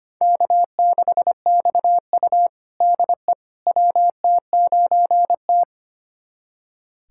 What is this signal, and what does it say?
Morse code, heard as a keyed tone, about 25 wpm, 700 Hz. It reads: K6XU DE WT9T
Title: USA Callsign DE Callsign - 4 Characters - 25wpm